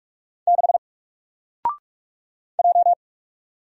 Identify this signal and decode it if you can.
Morse code, heard as a keyed tone, about 44 wpm, 700 Hz. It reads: 6 J
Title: Single Letter-Number 45wpm